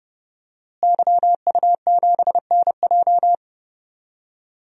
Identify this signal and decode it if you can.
Morse code, heard as a keyed tone, about 31 wpm, 700 Hz. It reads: YU7NJ
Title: DXCC-100 Call Signs - 5 Characters - Repeated 2x - 30wpm